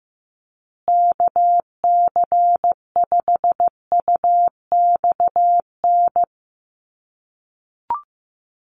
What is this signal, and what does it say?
Morse code, heard as a keyed tone, about 15 wpm, 700 Hz. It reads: KC5UXN
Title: USA Call Signs Repeated 2x 15wpm